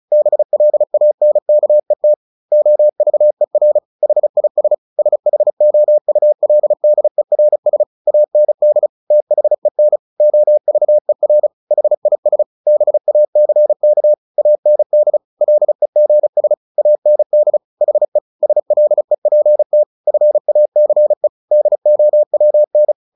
Morse code, heard as a keyed tone, about 34 wpm, 600 Hz. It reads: BLANKET OVER HIS SHOULDERS AND THEN OVER HIS BACK AND LEGS AND HE SLEPT FACE DOWN
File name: the old man and the sea 2 35wpm